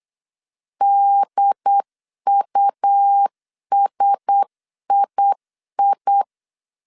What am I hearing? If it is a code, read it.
Morse code, heard as a keyed tone, about 8 wpm, 800 Hz. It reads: DUSII
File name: CW / DET / 1 6 1500C S C 8wpm 5219